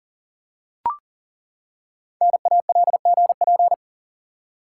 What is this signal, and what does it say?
Morse code, heard as a keyed tone, about 39 wpm, 700 Hz. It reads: NALGP
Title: Code Groups - 5 Letters - 40wpm